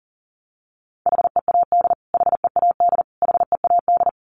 Morse code, heard as a keyed tone, about 40 wpm, 700 Hz. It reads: HEAD HEAD HEAD
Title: Top 500 Words Repeated 40wpm